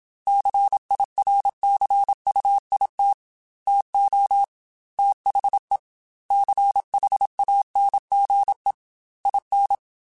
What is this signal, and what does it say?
Morse code, heard as a keyed tone, about 27 wpm, 800 Hz. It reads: CIRCUIT TO THE CHANGE IN